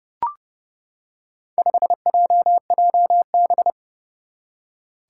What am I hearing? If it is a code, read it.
Morse code, heard as a keyed tone, about 30 wpm, 700 Hz. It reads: 5JJB
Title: Four Challenging Letter-Number Combinations 30wpm